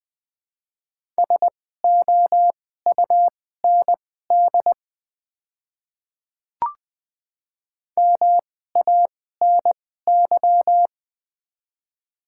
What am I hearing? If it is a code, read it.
Morse code, heard as a keyed tone, about 20 wpm, spaced as for 14 wpm, 700 Hz. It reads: SOUND MANY
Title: 2x Character Spacing - Top 100 Words - 20wpm